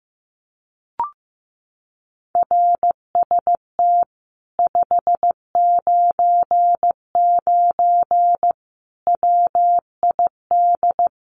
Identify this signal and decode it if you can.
Morse code, heard as a keyed tone, about 15 wpm, 700 Hz. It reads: RST 599 WID
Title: Common QSO Phrases Encore 15wpm